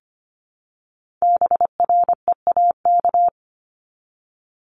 Morse code, heard as a keyed tone, about 25 wpm, 700 Hz. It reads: BREAK
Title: Multi-letter Phonemes Repeated 3x 25wpm